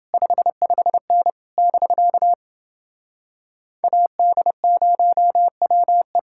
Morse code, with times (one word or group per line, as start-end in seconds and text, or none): 0.1-1.3 55N
1.6-2.3 <BK>
3.8-6.2 AD0WE